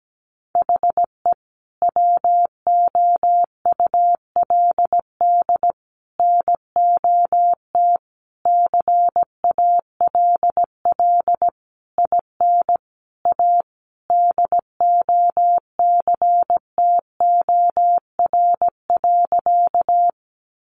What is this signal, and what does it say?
Morse code, heard as a keyed tone, about 17 wpm, 700 Hz. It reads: HE WOULD NOT CALL IN A DOCTOR.
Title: Sentences from Top 1500 Words 17wpm